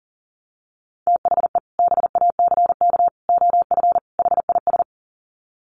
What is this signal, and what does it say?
Morse code, heard as a keyed tone, about 40 wpm, 700 Hz. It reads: THE BACK OF HIS
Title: Sets of 4 Words 40wpm